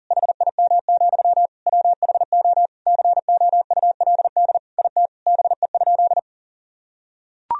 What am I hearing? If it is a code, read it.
Morse code, heard as a keyed tone, about 40 wpm, 700 Hz. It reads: HIM, WHO COULD IT BE?
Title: Sentences from Top 1000 Words - Review Part 2 - 40wpm